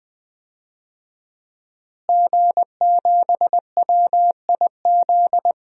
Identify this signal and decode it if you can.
Morse code, heard as a keyed tone, about 21 wpm, 700 Hz. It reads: G7WIZ
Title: DXCC-100 Call Signs - 5 Characters - Repeated 3x - 20wpm